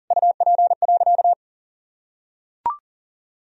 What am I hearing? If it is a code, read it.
Morse code, heard as a keyed tone, about 40 wpm, 700 Hz. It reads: UP.
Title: Sentences from Top 1000 Words - Review Part 1 - 40wpm